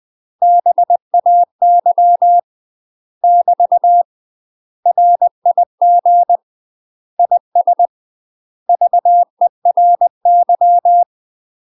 Morse code, heard as a keyed tone, about 20 wpm, 700 Hz. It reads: BAY <BT> RIG IS VERY
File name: UK MASTER QSOs TRAINING / 20WPM / UK MASTER QSOs 0032